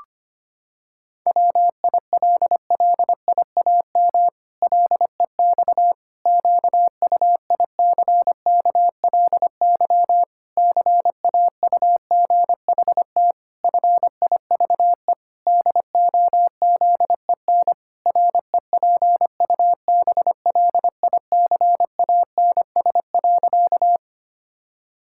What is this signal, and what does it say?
Morse code, heard as a keyed tone, about 25 wpm, 700 Hz. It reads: WILLIAM LEX QUICKLY CAUGHT FIVE DOZEN REPUBLICANS.
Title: Pangram Challenge v2 - 25wpm